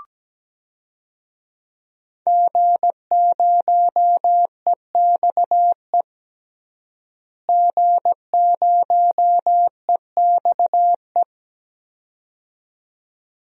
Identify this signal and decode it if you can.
Morse code, heard as a keyed tone, about 17 wpm, 700 Hz. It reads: G0EXE G0EXE
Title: DXCC-100 Call Signs - 5 Characters - Repeated 2x - 17wpm